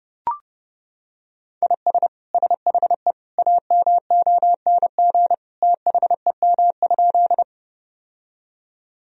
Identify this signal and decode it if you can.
Morse code, heard as a keyed tone, about 30 wpm, 700 Hz. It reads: IS SHE AMONG THEM?
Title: Sentences from Top 1000 Words - Review Part 3 - 30wpm